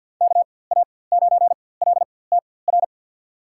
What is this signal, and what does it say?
Morse code, heard as a keyed tone, about 52 wpm, spaced as for 23 wpm, 700 Hz. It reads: KA9LTR
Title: USA Call Signs - 6 Characters - 4x Character Spacing - 50wpm